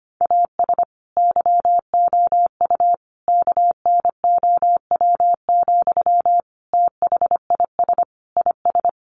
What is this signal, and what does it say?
Morse code, heard as a keyed tone, about 25 wpm, 700 Hz. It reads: AS YOU KNOW, THIS IS
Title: Sentences from Top 1100 Words 25wpm